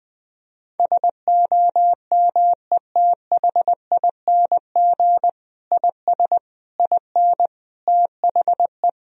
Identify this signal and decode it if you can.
Morse code, heard as a keyed tone, about 20 wpm, 700 Hz. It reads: SOMETHING IS IN THE
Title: Sentences from Top 1000 Words - Review Part 1 - 20wpm